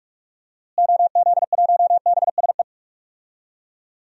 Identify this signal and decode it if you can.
Morse code, heard as a keyed tone, about 46 wpm, 700 Hz. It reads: OZ1BSE
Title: DXCC-100 Call Signs - 6 Characters - Repeated 3x - 45wpm